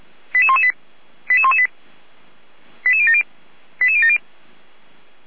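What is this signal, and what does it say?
Five-tone selective call (ZVEI), heard as quick runs of five tones, tones of about 80 ms. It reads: 88118 88118 88070 88070
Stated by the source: ZVEI five-tone selective call 88383 received via SDR.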